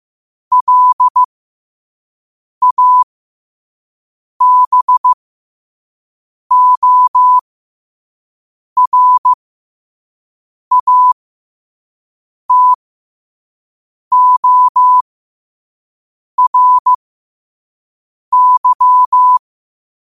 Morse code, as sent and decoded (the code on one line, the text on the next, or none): .-.. .- -... --- .-. .- - --- .-. -.--
LABORATORY